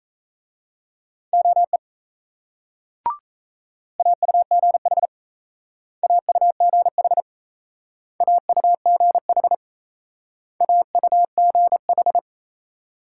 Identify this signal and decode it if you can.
Morse code, heard as a keyed tone, about 35 wpm, 700 Hz. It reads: OE AUGH AUGH AUGH AUGH
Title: Speed-Racing Multi-letter Phonemes - 28wpm